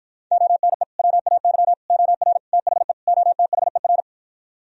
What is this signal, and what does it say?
Morse code, heard as a keyed tone, about 52 wpm, 700 Hz. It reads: ONE WAY OR THE OTHER